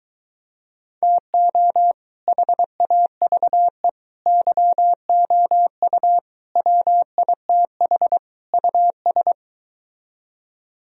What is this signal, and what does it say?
Morse code, heard as a keyed tone, about 22 wpm, 700 Hz. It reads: TO HAVE YOU WITH US